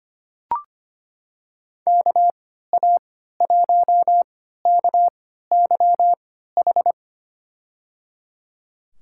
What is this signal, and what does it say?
Morse code, heard as a keyed tone, about 25 wpm, spaced as for 14 wpm, 700 Hz. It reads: KA1KYH
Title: USA Call Signs - 3x Character Spacing - 25wpm